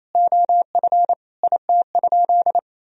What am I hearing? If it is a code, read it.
Morse code, heard as a keyed tone, about 28 wpm, 700 Hz. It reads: OF IT?
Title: Sentences from Top 100 Words and Abbreviations 28wpm